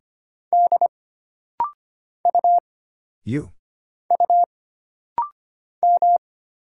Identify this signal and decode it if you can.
Morse code, heard as a keyed tone, about 25 wpm, 700 Hz. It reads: D U U M
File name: Single Character (Letter:Number:Punctuation:Prosign) - Rapid-Fire - 25wpm